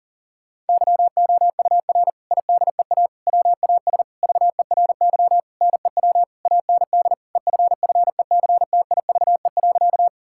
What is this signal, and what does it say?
Morse code, heard as a keyed tone, about 40 wpm, 700 Hz. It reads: YOUR IDEA WAS VERY NEW AND EFFECTIVE.
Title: Sentences from Top 1000 Words - Review Part 6 - 40wpm